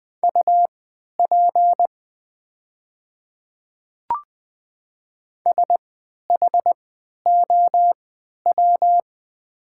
Morse code, heard as a keyed tone, about 20 wpm, spaced as for 11 wpm, 700 Hz. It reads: UP SHOW
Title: 3x Character Spacing - Top 200 Words - 20wpm